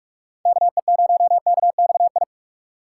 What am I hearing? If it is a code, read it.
Morse code, heard as a keyed tone, about 59 wpm, spaced as for 52 wpm, 700 Hz. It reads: KE0KXI
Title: Speed Racing - USA 2x3 Call Signs 45wpm